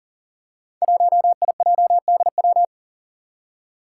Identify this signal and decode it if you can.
Morse code, heard as a keyed tone, about 37 wpm, 700 Hz. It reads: 1IJDW
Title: Five Challenging Letter-Number Combinations 40wpm